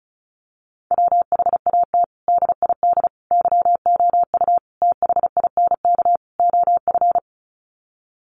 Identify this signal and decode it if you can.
Morse code, heard as a keyed tone, about 34 wpm, 700 Hz. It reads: WHAT DID YOU THINK OF
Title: Sets of 5 Words Encore2 35wpm